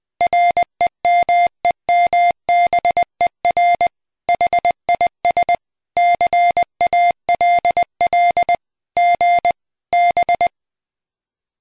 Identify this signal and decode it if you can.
Morse code, heard as a keyed tone, about 20 wpm, 700 Hz. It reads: REMEMBER HIS CALL G B